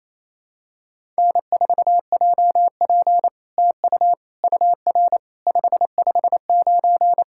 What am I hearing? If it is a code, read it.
Morse code, heard as a keyed tone, about 27 wpm, 700 Hz. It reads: N4JP TU UR 559
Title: SOTA QSO Practice 28wpm